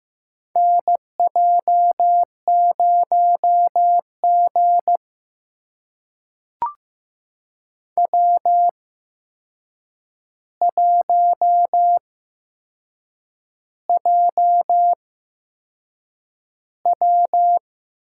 Morse code, heard as a keyed tone, about 15 wpm, spaced as for 6 wpm, 700 Hz. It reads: NJ0G W1JW
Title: USA Call Signs - 4 Characters - 8x Character Spacing - 15wpm